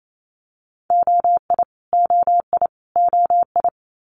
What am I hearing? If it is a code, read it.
Morse code, heard as a keyed tone, about 28 wpm, 700 Hz. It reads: OI OI OI
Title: Multi-letter Phonemes Repeated 3x 28wpm